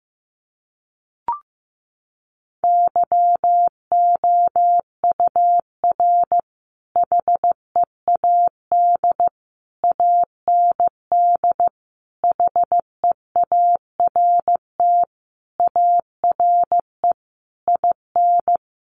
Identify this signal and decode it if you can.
Morse code, heard as a keyed tone, about 15 wpm, 700 Hz. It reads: YOUR HEAD AND HEART ARE IN
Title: Sentences from Top 1300 Words 15wpm